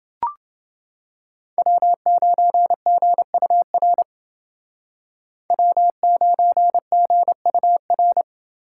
Morse code, heard as a keyed tone, about 29 wpm, 700 Hz. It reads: W9GUR W9GUR